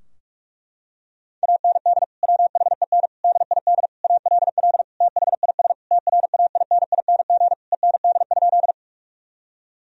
Morse code, heard as a keyed tone, about 46 wpm, 700 Hz. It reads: AND WHEN DID ALL THIS TRAINING END?
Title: Sentences from Top 1000 Words - Review Part 4 - 45wpm